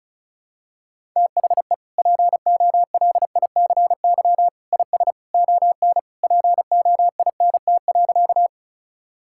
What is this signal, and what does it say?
Morse code, heard as a keyed tone, about 34 wpm, 700 Hz. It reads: THE POLICY IS ON POINT.
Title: Sentences from Top 1000 Words - Review Part 5 - 35wpm